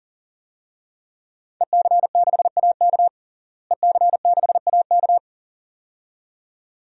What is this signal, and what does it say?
Morse code, heard as a keyed tone, about 37 wpm, 700 Hz. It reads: EC6AK EC6AK
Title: DXCC-100 Call Signs - 5 Characters - Repeated 2x - 40wpm